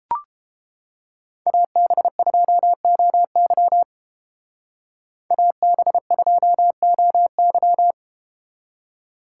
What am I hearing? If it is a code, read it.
Morse code, heard as a keyed tone, about 31 wpm, 700 Hz. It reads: AB2OY AB2OY